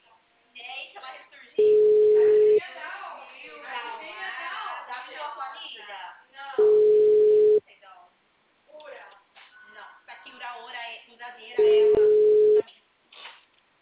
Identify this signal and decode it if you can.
Ringback tone (Europe, 425 Hz), with speech.